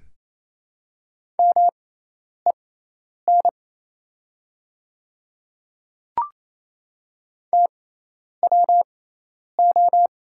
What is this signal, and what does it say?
Morse code, heard as a keyed tone, about 27 wpm, spaced as for 10 wpm, 700 Hz. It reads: MEN TWO
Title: 6x Character Spacing - Top 200 Words - 28wpm